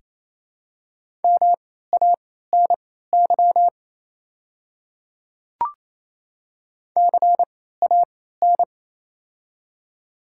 Morse code, heard as a keyed tone, about 28 wpm, spaced as for 16 wpm, 700 Hz. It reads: MANY CAN
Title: 3x Character Spacing - Top 100 Words - 28wpm